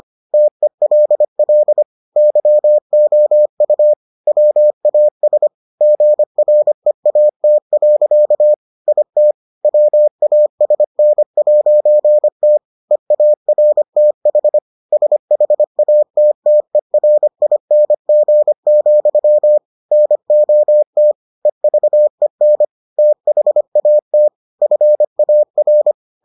Morse code, heard as a keyed tone, about 25 wpm, 600 Hz. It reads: TELL YOU WAS GREAT. IT WASN'T EARTH SHATTERING, NOT EVEN THAT FAR